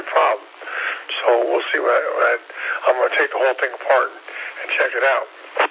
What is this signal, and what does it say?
Emergency Alert System (SAME) tones, heard as harsh buzzing bursts, with speech. It is heard from 0.6 to 4.6 s.